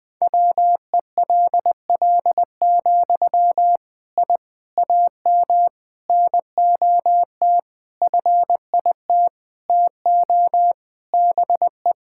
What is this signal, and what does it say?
Morse code, heard as a keyed tone, about 20 wpm, 700 Hz. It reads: WELL, I AM NOT FIT TO BE
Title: Sentences from Top 1000 Words - Review Part 5 - 20wpm